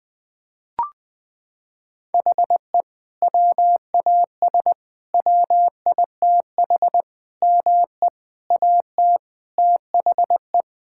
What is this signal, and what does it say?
Morse code, heard as a keyed tone, about 20 wpm, 700 Hz. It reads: HE WAS WITH ME AT THE
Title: Sentences from Top 1000 Words - Review Part 5 - 20wpm